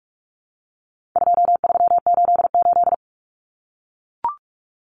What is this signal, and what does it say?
Morse code, heard as a keyed tone, about 44 wpm, 700 Hz. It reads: 2388
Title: Numbers - 4-Digit 45wpm